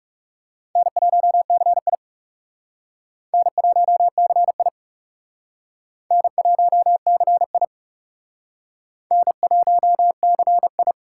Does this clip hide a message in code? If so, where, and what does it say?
Yes: Morse code from 0.8 to 10.9 s, heard as a keyed tone, about 40 wpm, 700 Hz. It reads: N1CI N1CI N1CI N1CI